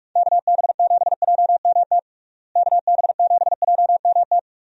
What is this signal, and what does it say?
Morse code, heard as a keyed tone, about 46 wpm, 700 Hz. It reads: KB7JMT KB7JMT
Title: USA Call Signs - 6 Characters - Repeated 3x - 45wpm